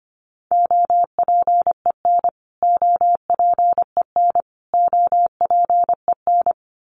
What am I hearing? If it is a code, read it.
Morse code, heard as a keyed tone, about 25 wpm, 700 Hz. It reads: OPEN OPEN OPEN
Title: Top 500 Words Repeated 25wpm